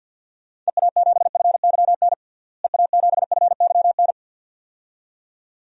Morse code, heard as a keyed tone, about 51 wpm, 700 Hz. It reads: EA7FYN EA7FYN
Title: DXCC-100 Call Signs - 6 Characters - Repeated 2x - 50wpm